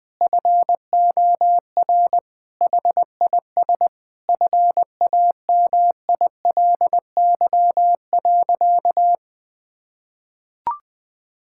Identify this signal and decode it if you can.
Morse code, heard as a keyed tone, about 20 wpm, 700 Hz. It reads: FOR HIS FAMILY.
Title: Sentences from Top 1000 Words - Review Part 5 - 20wpm